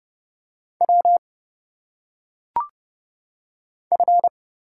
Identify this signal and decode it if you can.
Morse code, heard as a keyed tone, about 30 wpm, 700 Hz. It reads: W F
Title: Single Letters - Phonetic Pronunciation 30wpm